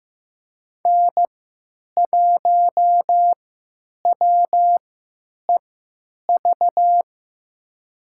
Morse code, heard as a keyed tone, about 15 wpm, spaced as for 9 wpm, 700 Hz. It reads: N1WEV
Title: USA Call Signs - 5 Characters - 3x Character Spacing - 15wpm